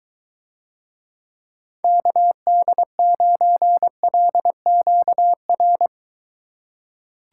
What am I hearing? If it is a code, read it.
Morse code, heard as a keyed tone, about 23 wpm, 700 Hz. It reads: KD9LQR